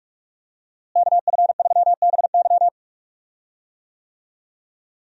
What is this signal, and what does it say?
Morse code, heard as a keyed tone, about 45 wpm, 700 Hz. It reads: KF3BY
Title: USA Call Signs - 5 Characters - Repeated 3x - 45wpm